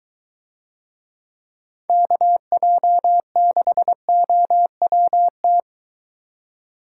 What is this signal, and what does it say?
Morse code, heard as a keyed tone, about 23 wpm, 700 Hz. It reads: KJ6OWT